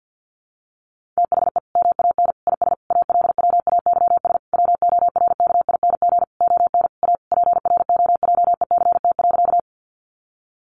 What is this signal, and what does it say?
Morse code, heard as a keyed tone, about 50 wpm, 700 Hz. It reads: THE MAN IS ALWAYS WORKING ON A PROJECT.
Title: Sentences from Top 1400 Words 50wpm